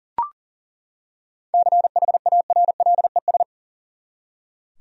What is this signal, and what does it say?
Morse code, heard as a keyed tone, about 40 wpm, 700 Hz. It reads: CHARLES
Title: USA Common Names - 40wpm